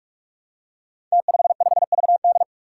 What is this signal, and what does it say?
Morse code, heard as a keyed tone, about 44 wpm, 700 Hz. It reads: T55VD
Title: Five Challenging Letter-Number Combinations 45wpm